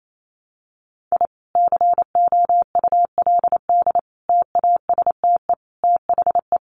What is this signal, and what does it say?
Morse code, heard as a keyed tone, about 28 wpm, 700 Hz. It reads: I COULD TASTE THE
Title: Sentences from Top 1200 Words 28wpm